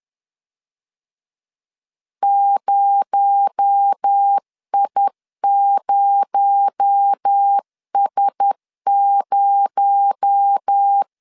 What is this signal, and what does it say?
Morse code, heard as a keyed tone, about 10 wpm, 800 Hz. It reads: 0I0S0
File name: LEZIONI / 6 PKX0 / 6 PKX0 10wpm 2707